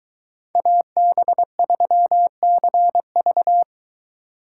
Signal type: Morse code

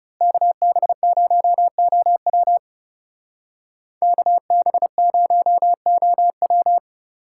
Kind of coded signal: Morse code

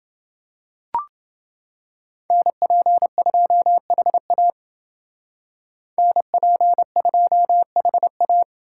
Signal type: Morse code

